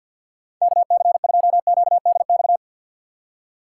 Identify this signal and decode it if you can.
Morse code, heard as a keyed tone, about 51 wpm, 700 Hz. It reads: KC2QDX
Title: USA Call Signs - 4x Character Spacing - 50wpm